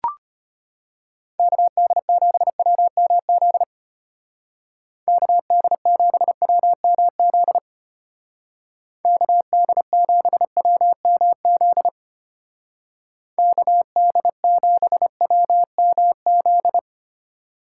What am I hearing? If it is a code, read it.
Morse code, heard as a keyed tone, about 32 wpm, 700 Hz. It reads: KD7WMZ KD7WMZ KD7WMZ KD7WMZ